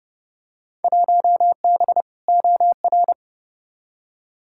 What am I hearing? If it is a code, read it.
Morse code, heard as a keyed tone, about 29 wpm, 700 Hz. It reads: 1B OR
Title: Field Day Exchanges - 30wpm